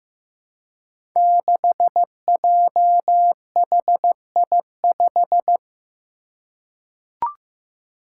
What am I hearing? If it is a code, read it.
Morse code, heard as a keyed tone, about 15 wpm, 700 Hz. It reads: 6JHI5
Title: Five Challenging Letter-Number Combinations 15wpm